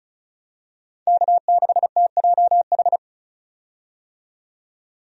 Morse code, heard as a keyed tone, about 34 wpm, 700 Hz. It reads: K6TJH
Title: USA Call Signs - 5 Characters - 35wpm